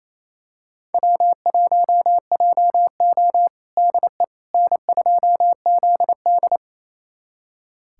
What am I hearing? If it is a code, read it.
Morse code, heard as a keyed tone, about 28 wpm, 700 Hz. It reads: W1JO DE N2ZD
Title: USA Callsign DE Callsign - 4 Characters - 28wpm